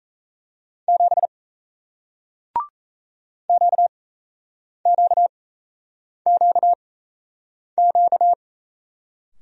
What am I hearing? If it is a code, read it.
Morse code, heard as a keyed tone, about 37 wpm, spaced as for 8 wpm, 700 Hz. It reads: 7 QQQQ